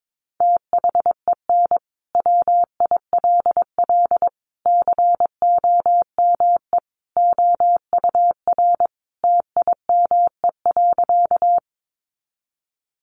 Morse code, heard as a keyed tone, about 22 wpm, 700 Hz. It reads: THEN WILL COME OUR TIME.
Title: Sentences from Top 200 Words 22wpm